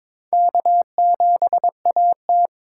Morse code, heard as a keyed tone, about 22 wpm, 700 Hz. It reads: K7AT
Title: USA Call Signs - 4 Characters - 8x Character Spacing - 22wpm